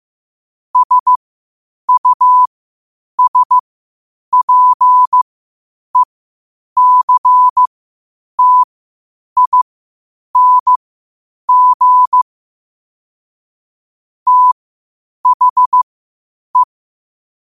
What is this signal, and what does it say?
Morse code, heard as a keyed tone, about 15 wpm, spaced as for 8 wpm, 1000 Hz. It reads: SUSPECTING THE
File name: qotdfiles / 8wpm